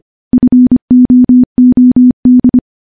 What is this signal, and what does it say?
Morse code, heard as a keyed tone, about 25 wpm, 250 Hz. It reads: FOOD
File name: morse / morse-training-25WPM